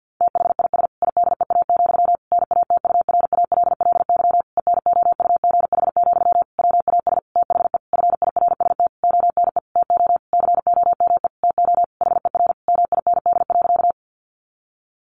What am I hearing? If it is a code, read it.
Morse code, heard as a keyed tone, about 50 wpm, 700 Hz. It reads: THIS IDEA, NATURALLY ENOUGH, WAS THE FIRST ONE TO COME TO HER MIND.
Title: Sentences from Top 1300 Words 50wpm